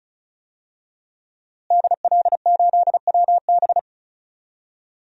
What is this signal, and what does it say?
Morse code, heard as a keyed tone, about 34 wpm, 700 Hz. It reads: DL8WB
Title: DXCC-100 Call Signs - 5 Characters - Repeated 2x - 35wpm